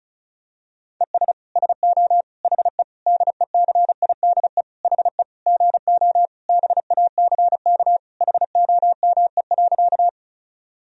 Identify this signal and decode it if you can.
Morse code, heard as a keyed tone, about 36 wpm, 700 Hz. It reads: ES SO HE DECIDE HE GO BACK HOME.
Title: Sentences from Top 500 Words and Abbreviations 35wpm